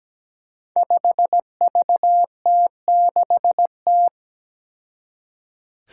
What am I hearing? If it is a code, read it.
Morse code, heard as a keyed tone, about 17 wpm, 700 Hz. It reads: 5VT6T
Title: Five Challenging Letter-Number Combinations 17wpm